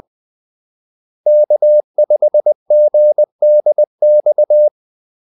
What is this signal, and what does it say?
Morse code, heard as a keyed tone, about 20 wpm, 600 Hz. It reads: K5GDX
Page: Kaldesignaler 20 wpm